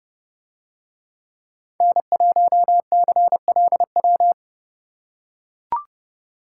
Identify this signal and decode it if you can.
Morse code, heard as a keyed tone, about 29 wpm, 700 Hz. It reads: N1CLW